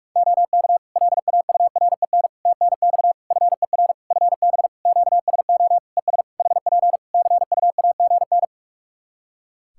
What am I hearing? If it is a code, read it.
Morse code, heard as a keyed tone, about 46 wpm, 700 Hz. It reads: OK LAUREN TNX FER FB QSO ES HP CUAGN
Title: Practice QSO - Exchange 4 - Ending - 45wpm